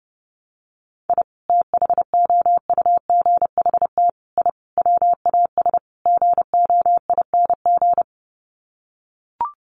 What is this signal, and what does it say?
Morse code, heard as a keyed tone, about 30 wpm, 700 Hz. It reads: I THOUGHT I WAS GOING
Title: Sets of 5 Words Encore2 30wpm